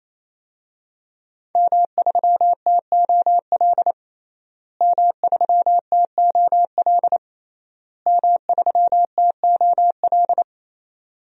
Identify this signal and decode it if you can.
Morse code, heard as a keyed tone, about 28 wpm, 700 Hz. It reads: M3TOL M3TOL M3TOL
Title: DXCC-100 Call Signs - 5 Characters - Repeated 3x - 28wpm